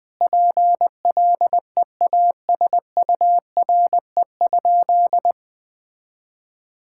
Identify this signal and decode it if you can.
Morse code, heard as a keyed tone, about 21 wpm, 700 Hz. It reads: PLEASURE?
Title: Sentences from Top 1600 Words 20wpm